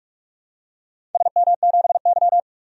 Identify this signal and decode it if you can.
Morse code, heard as a keyed tone, about 46 wpm, 700 Hz. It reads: SM7Y
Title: DXCC-100 Call Signs - 4 Characters - Repeated 2x - 45wpm